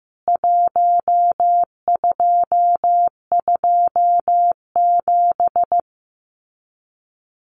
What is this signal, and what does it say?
Morse code, heard as a keyed tone, about 15 wpm, 700 Hz. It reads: 1227
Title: CWT Exchange 15wpm